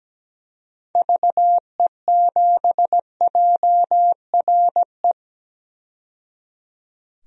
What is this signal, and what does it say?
Morse code, heard as a keyed tone, about 17 wpm, 700 Hz. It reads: VE7JRE
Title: DXCC-100 Call Signs - 6 Characters - 17wpm